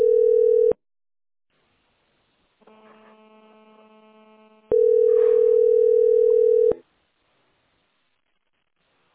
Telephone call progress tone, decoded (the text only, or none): ringback tone